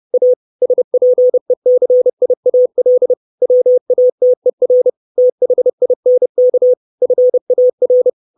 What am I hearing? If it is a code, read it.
Morse code, heard as a keyed tone, about 30 wpm, 500 Hz. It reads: A SPECIAL WATER THINK FAR
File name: 632 Words 30WPM